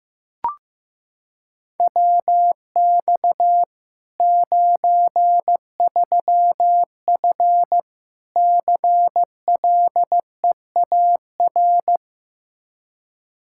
Morse code, heard as a keyed tone, about 15 wpm, 700 Hz. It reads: WX 93F CLEAR
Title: Practice QSO - Exchange 2 - Info - 15wpm